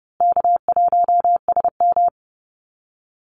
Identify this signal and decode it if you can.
Morse code, heard as a keyed tone, about 30 wpm, 700 Hz. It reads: K1SM
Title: Easy Call Signs Repeated 2x - 30wpm